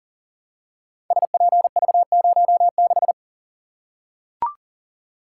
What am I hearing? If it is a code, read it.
Morse code, heard as a keyed tone, about 39 wpm, 700 Hz. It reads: SPV06
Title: Code Groups - 5 Letter:Numbers - 40wpm